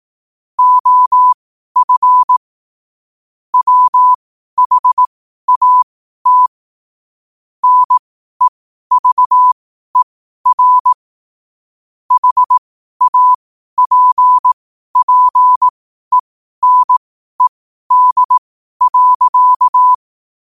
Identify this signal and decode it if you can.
Morse code, heard as a keyed tone, about 18 wpm, spaced as for 12 wpm, 1000 Hz. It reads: OF WHAT NEVER HAPPENED.
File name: qotdfiles / 12wpm